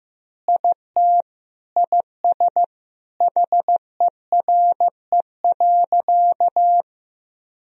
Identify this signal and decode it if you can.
Morse code, heard as a keyed tone, about 14 wpm, 700 Hz. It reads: IT IS HERE.
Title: Sentences from Top 1000 Words - Review Part 2 - 15wpm